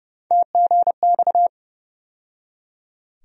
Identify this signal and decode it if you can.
Morse code, heard as a keyed tone, about 30 wpm, 700 Hz. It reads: TGX
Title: USA Call Sign Postfixes - 3 Letters 30wpm